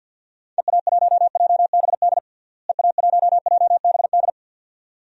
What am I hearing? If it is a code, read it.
Morse code, heard as a keyed tone, about 51 wpm, 700 Hz. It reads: EA1JBD EA1JBD
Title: DXCC-100 Call Signs - 6 Characters - Repeated 2x - 50wpm